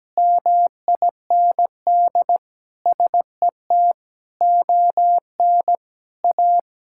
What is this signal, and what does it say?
Morse code, heard as a keyed tone, about 17 wpm, 700 Hz. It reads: MIND SET ON A
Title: Sentences from Top 1000 Words - Review Part 5 - 17wpm